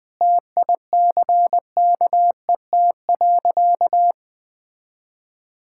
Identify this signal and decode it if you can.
Morse code, heard as a keyed tone, about 21 wpm, 700 Hz. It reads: TICKET.
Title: Sentences with 6-Word Phrases - 20wpm